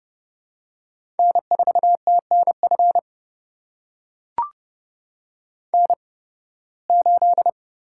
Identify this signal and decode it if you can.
Morse code, heard as a keyed tone, about 30 wpm, spaced as for 20 wpm, 700 Hz. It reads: N4TNF N8
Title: USA Call Signs - 5 Characters - 8x Character Spacing - 30wpm